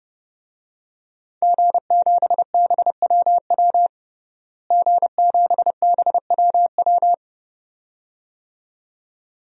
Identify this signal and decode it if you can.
Morse code, heard as a keyed tone, about 30 wpm, 700 Hz. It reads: G7BWW G7BWW
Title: DXCC-100 Call Signs - 5 Characters - Repeated 2x - 30wpm